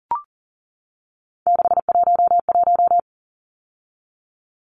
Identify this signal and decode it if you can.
Morse code, heard as a keyed tone, about 40 wpm, 700 Hz. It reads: 611
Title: Numbers - 3-Digit 40wpm